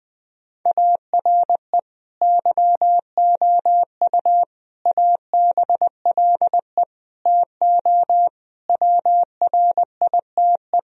Morse code, heard as a keyed tone, about 20 wpm, 700 Hz. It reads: ARE YOU ABLE TO WRITE
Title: Sentences from Top 1000 Words - Review Part 3 - 20wpm